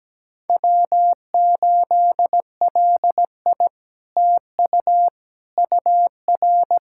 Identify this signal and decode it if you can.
Morse code, heard as a keyed tone, about 17 wpm, 700 Hz. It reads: W8LI TU UR
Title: SOTA QSO Practice 17wpm